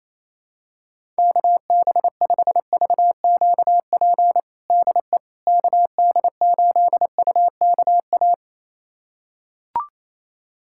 Morse code, heard as a keyed tone, about 28 wpm, 700 Hz. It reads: KB5VQP DE KD8UKA
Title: USA Callsign DE Callsign - 6 Characters - 28wpm